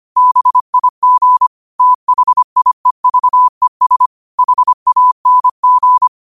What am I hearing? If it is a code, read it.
Morse code, heard as a keyed tone, about 25 wpm, 1000 Hz. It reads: BIG THIEVES HANG